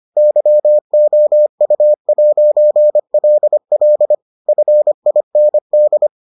Morse code, heard as a keyed tone, about 25 wpm, 600 Hz. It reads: YOU'LL FIND